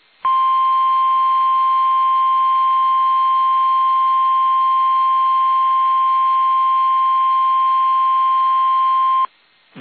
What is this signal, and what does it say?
Emergency Alert System (SAME) tones (the 1050 Hz attention tone). It is heard from 0.2 to 9.3 s.
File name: eas / KUSO / KUSO